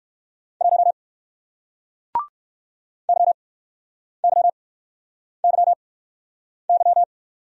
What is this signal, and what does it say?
Morse code, heard as a keyed tone, about 54 wpm, spaced as for 9 wpm, 700 Hz. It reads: 1 YYYY
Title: Speed-Racing Single Letter-Number 45wpm